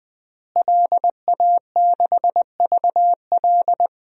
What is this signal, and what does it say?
Morse code, heard as a keyed tone, about 20 wpm, 700 Hz. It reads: LA6VL
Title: DXCC-100 Call Signs - 5 Characters - Repeated 3x - 20wpm